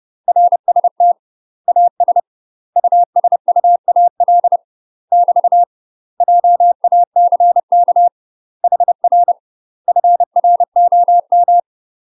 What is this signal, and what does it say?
Morse code, heard as a keyed tone, about 30 wpm, 700 Hz. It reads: RST AS USUAL <BT> JACK HR FROM
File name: UK MASTER QSOs TRAINING / 30WPM / UK MASTER QSOs 0050